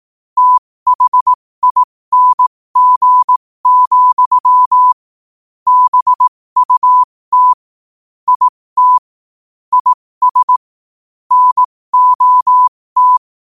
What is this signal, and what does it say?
Morse code, heard as a keyed tone, about 18 wpm, spaced as for 15 wpm, 1000 Hz. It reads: THING, BUT IT IS NOT
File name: 15wpm